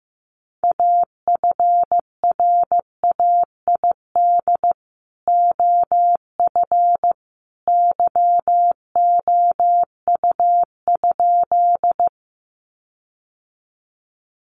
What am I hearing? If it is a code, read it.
Morse code, heard as a keyed tone, about 15 wpm, 700 Hz. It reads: AFRAID OF YOU?
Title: Sentences from Top 1500 Words 15wpm